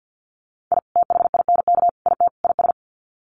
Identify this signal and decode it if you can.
Morse code, heard as a keyed tone, about 50 wpm, 700 Hz. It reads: I THINK IT IS
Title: Sets of 4 Words 50wpm